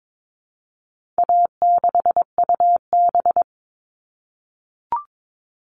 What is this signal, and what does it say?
Morse code, heard as a keyed tone, about 22 wpm, 700 Hz. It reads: A6UB
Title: All Cut Numbers - 4-Digit 22wpm